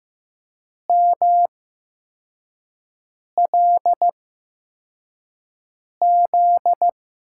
Morse code, heard as a keyed tone, about 15 wpm, spaced as for 4 wpm, 700 Hz. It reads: MLZ
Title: USA Call Signs - 8x Character Spacing - 15wpm